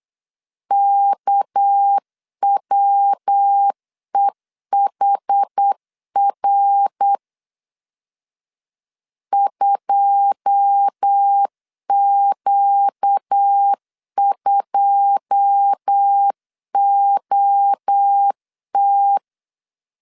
Morse code, heard as a keyed tone, about 8 wpm, 800 Hz. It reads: KWEHR 2Q2OT
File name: LEZIONI / 9 2 8wpm 3437